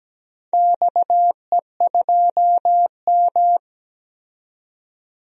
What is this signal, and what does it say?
Morse code, heard as a keyed tone, about 17 wpm, 700 Hz. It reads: XE2M
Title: DXCC-100 Call Signs - 4 Characters - 17wpm